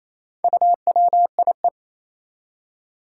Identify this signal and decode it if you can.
Morse code, heard as a keyed tone, about 28 wpm, 700 Hz. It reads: UWIE
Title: Four Challenging Letter-Number Combinations 28wpm